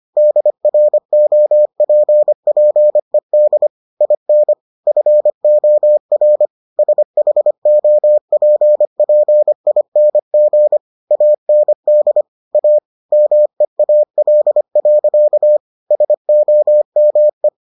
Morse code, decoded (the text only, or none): DROPPED IN FOR SHOPPING AND A MEAL. SOME